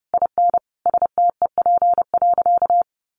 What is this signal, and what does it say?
Morse code, heard as a keyed tone, about 30 wpm, 700 Hz. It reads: IN STEP.
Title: Sentences from Top 400 Words 30wpm